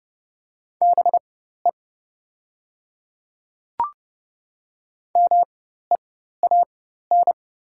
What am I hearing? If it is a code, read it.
Morse code, heard as a keyed tone, about 30 wpm, spaced as for 14 wpm, 700 Hz. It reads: BE MEAN
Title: 4x Character Spacing - Top 200 Words - 30wpm